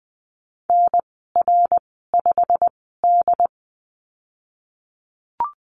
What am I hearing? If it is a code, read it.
Morse code, heard as a keyed tone, about 20 wpm, spaced as for 15 wpm, 700 Hz. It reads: NR5D
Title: 088 Lesson 15g - Number 5 - Call Signs 20wpm